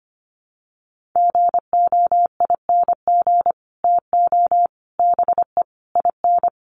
Morse code, heard as a keyed tone, about 25 wpm, 700 Hz. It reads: GOING TO BE IN
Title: Sets of 5 Words Encore 25wpm